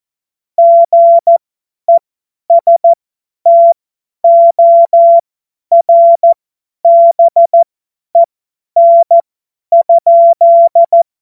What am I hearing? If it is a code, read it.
Morse code, heard as a keyed tone, about 14 wpm, spaced as for 10 wpm, 680 Hz. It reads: GESTORBEN?
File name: Morsekurs / Witze / Witz 26. Oktober 2021, 50 F70